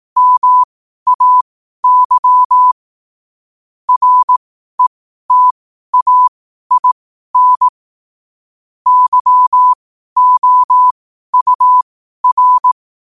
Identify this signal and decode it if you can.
Morse code, heard as a keyed tone, about 18 wpm, spaced as for 12 wpm, 1000 Hz. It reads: MAY RETAIN YOUR
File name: qotdfiles / 12wpm